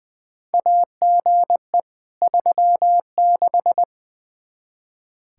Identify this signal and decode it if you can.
Morse code, heard as a keyed tone, about 21 wpm, 700 Hz. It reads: AGE 36
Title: Practice QSO - Exchange 3 - More Info - 20wpm